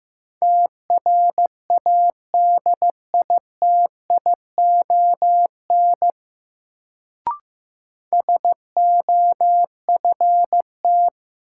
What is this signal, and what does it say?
Morse code, heard as a keyed tone, about 15 wpm, 700 Hz. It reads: TRADITION SOFT
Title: Sentences from Top 1500 Words - New Word Drill Alt - 15wpm